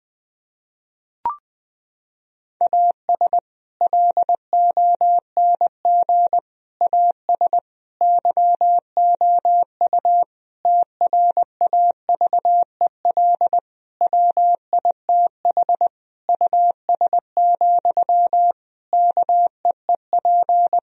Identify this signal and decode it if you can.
Morse code, heard as a keyed tone, about 20 wpm, 700 Hz. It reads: AS LONG AS YOU TRAVEL WITH US, KEEP
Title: Sentences from Top 1000 Words - Review Part 2 - 20wpm